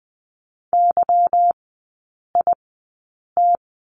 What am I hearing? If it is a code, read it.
Morse code, heard as a keyed tone, about 20 wpm, spaced as for 8 wpm, 700 Hz. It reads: YIT
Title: 152 Lesson 25e - Letter p - Three Characters 20wpm